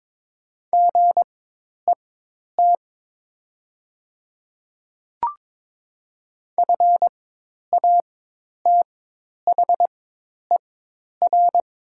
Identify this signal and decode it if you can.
Morse code, heard as a keyed tone, about 22 wpm, spaced as for 10 wpm, 700 Hz. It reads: GET FATHER
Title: 4x Character Spacing - Top 200 Words - 22wpm